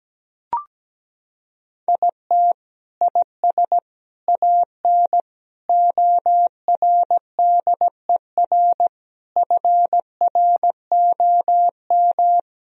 Morse code, heard as a keyed tone, about 17 wpm, 700 Hz. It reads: IT IS AN ORDER FROM
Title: Sentences from Top 1000 Words - Review Part 2 - 17wpm